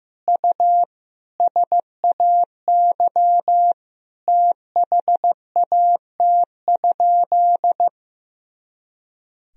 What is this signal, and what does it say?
Morse code, heard as a keyed tone, about 15 wpm, 700 Hz. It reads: U SAY THAT?
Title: Sentences from Top 200 Words and Abbreviations 15wpm